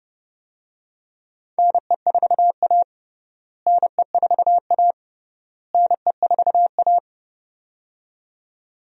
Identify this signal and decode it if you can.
Morse code, heard as a keyed tone, about 30 wpm, 700 Hz. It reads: NE4A NE4A NE4A
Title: USA Call Signs - 4 Characters - Repeated 3x - 30wpm